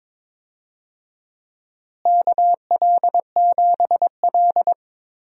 Morse code, heard as a keyed tone, about 22 wpm, 700 Hz. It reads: KL7L
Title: DXCC-100 Call Signs - 4 Characters - Repeated 3x - 22wpm